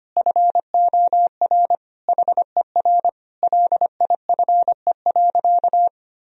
Morse code, heard as a keyed tone, about 25 wpm, 700 Hz. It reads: FOR HER LIFE.
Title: Sentences from Top 1000 Words - Review Part 4 - 25wpm